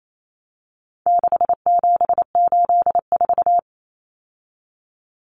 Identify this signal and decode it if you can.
Morse code, heard as a keyed tone, about 28 wpm, 700 Hz. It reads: 6784
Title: Common Cut Numbers - 4-Digit 28wpm